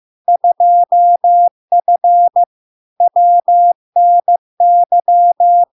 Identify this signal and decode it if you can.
Morse code, heard as a keyed tone, about 15 wpm, 700 Hz. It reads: 2F WNY
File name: LARK-FD-2022-Log-in-CW-15wpm-700-Hz